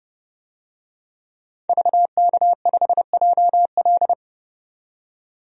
Morse code, heard as a keyed tone, about 30 wpm, 700 Hz. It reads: VK5JL
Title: DXCC-100 Call Signs - 5 Characters - 30wpm